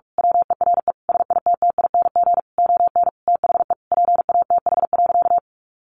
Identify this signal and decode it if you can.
Morse code, heard as a keyed tone, about 46 wpm, 700 Hz. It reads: WERE SITTING ON THE PATH.
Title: A Sound Of Thunder 45wpm